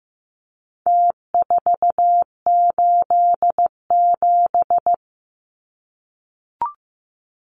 Morse code, heard as a keyed tone, about 15 wpm, 700 Hz. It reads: T487
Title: Numbers - Contest Style - 4-Digit 15wpm